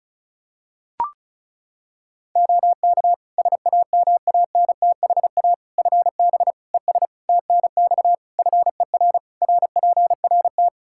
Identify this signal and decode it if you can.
Morse code, heard as a keyed tone, about 34 wpm, 700 Hz. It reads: OK SAMANTHA FB ES TNX FER RPRT
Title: Practice QSO - Exchange 2 - Info - 35wpm